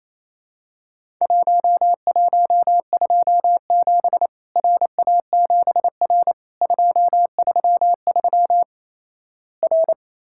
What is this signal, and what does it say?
Morse code, heard as a keyed tone, about 28 wpm, 700 Hz. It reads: 1127 RA7R 233 R
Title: Worked All Europe DX Contest - QTC Exchange - 28wpm